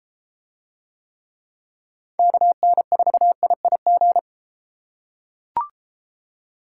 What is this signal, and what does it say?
Morse code, heard as a keyed tone, about 33 wpm, 700 Hz. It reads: KN4IIG